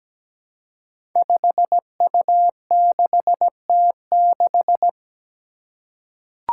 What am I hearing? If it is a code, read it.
Morse code, heard as a keyed tone, about 17 wpm, 700 Hz. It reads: 5U6T6
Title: Five Challenging Letter-Number Combinations 17wpm